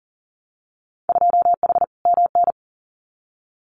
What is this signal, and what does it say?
Morse code, heard as a keyed tone, about 38 wpm, 700 Hz. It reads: 2H MN
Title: Winter Field Day Exchanges 40wpm